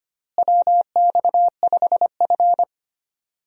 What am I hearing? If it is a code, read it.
Morse code, heard as a keyed tone, about 25 wpm, 700 Hz. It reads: WX5F
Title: USA Call Signs - 4 Characters - Repeated 3x - 25wpm